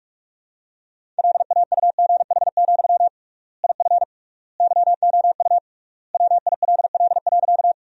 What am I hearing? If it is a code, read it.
Morse code, heard as a keyed tone, about 46 wpm, 700 Hz. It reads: LAUGH, IF YOU WILL.
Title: 3x Word-Spacing - Sentences from Top 500 Words - 45wpm